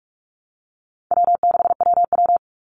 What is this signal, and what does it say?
Morse code, heard as a keyed tone, about 44 wpm, 700 Hz. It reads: W6WW
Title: NA SOTA Chaser Callsigns 45wpm